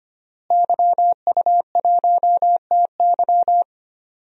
Morse code, heard as a keyed tone, about 25 wpm, 700 Hz. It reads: YU1TY
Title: DXCC-100 Call Signs - 5 Characters - Repeated 2x - 25wpm